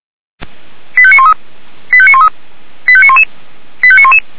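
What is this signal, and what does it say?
Five-tone selective call (ZVEI), heard as quick runs of five tones, tones of about 70 ms. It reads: 85913 85913 85911 85911
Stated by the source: ZVEI five-tone selective call 85381 received by SDR.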